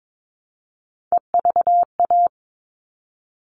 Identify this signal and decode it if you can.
Morse code, heard as a keyed tone, about 22 wpm, 700 Hz. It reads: EVA
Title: All Cut Numbers - 3-Digit 22wpm